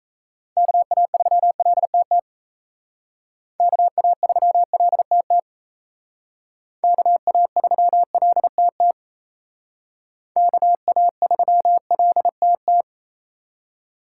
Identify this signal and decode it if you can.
Morse code, heard as a keyed tone, about 33 wpm, 700 Hz. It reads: KA3LTT KA3LTT KA3LTT KA3LTT
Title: Speed Racing - USA 2x3 Call Signs 28wpm